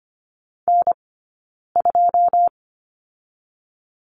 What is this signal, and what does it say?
Morse code, heard as a keyed tone, about 24 wpm, spaced as for 9 wpm, 700 Hz. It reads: N2
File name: Lesson 15d - Number 5 - Two Characters 25wpm